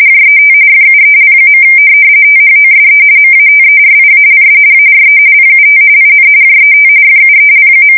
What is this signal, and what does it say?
RTTY, about 45.45 baud. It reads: JET FUEL. THOSE MATERIALS ARE PUT IN JET FUEL.
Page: 170 Shift 2125/2295